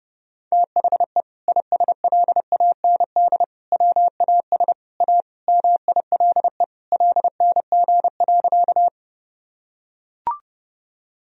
Morse code, heard as a keyed tone, about 32 wpm, 700 Hz. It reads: THE ISLAND WAS A MILE LNG.
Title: Sentences from Top 500 Words and Abbreviations 30wpm